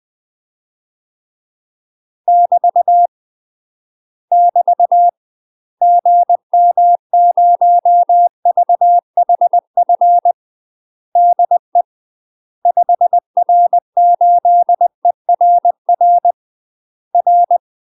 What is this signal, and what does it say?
Morse code, heard as a keyed tone, about 20 wpm, 700 Hz. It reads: <BT> <BT> GM0VHF DE 5R8ERR R
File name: UK MASTER QSOs TRAINING / 20WPM / UK MASTER QSOs 0025